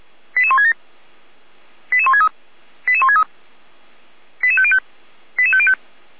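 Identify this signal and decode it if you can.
Five-tone selective call (ZVEI), heard as quick runs of five tones, tones of about 70 ms. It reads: 88167 88163 88163 88505 88505
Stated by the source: ZVEI five-tone call 88169 received on an SDR, channel 455, from Feuerwehr Orlamünde.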